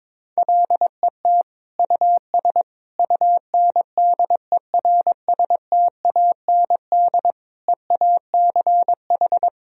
Morse code, heard as a keyed tone, about 22 wpm, 700 Hz. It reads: LET US UNDERSTAND EACH
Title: Sentences from Top 1000 Words - Review Part 1 - 22wpm